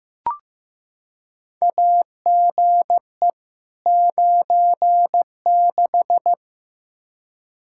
Morse code, heard as a keyed tone, about 15 wpm, 700 Hz. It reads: AGE 96
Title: Practice QSO - Exchange 3 - More Info - 15wpm